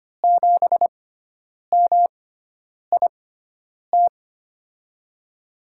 Morse code, heard as a keyed tone, about 25 wpm, spaced as for 9 wpm, 700 Hz. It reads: 7MIT
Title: USA Call Signs - 6x Character Spacing - 25wpm